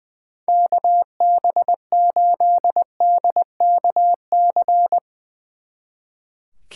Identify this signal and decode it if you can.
Morse code, heard as a keyed tone, about 22 wpm, 700 Hz. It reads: KB8DKC
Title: Speed Racing - USA 2x3 Call Signs 20wpm